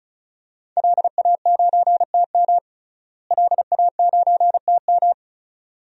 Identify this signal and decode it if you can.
Morse code, heard as a keyed tone, about 36 wpm, 700 Hz. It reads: LA9TM LA9TM
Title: DXCC-100 Call Signs - 5 Characters - Repeated 2x - 35wpm